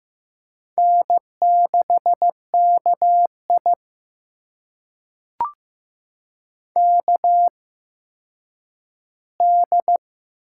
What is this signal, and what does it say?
Morse code, heard as a keyed tone, about 15 wpm, spaced as for 9 wpm, 700 Hz. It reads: N6KI KD
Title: USA Call Signs - 8x Character Spacing - 15wpm